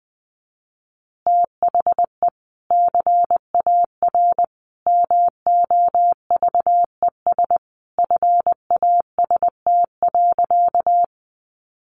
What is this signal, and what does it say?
Morse code, heard as a keyed tone, about 20 wpm, 700 Hz. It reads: THE CAR MOVES FAST.
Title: Easy Phrases 20wpm